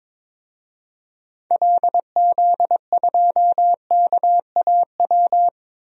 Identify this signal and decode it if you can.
Morse code, heard as a keyed tone, about 21 wpm, 700 Hz. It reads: LZ2KAW
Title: DXCC-100 Call Signs - 6 Characters - Repeated 2x - 22wpm